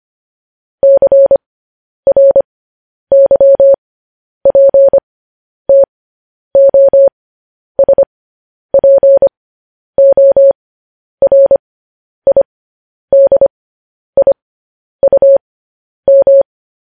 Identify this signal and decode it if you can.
Morse code, heard as a keyed tone, about 25 wpm, spaced as for 10 wpm, 550 Hz. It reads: CRYPTOSPORIDIUM
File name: podcast / News headlines-1x-25x10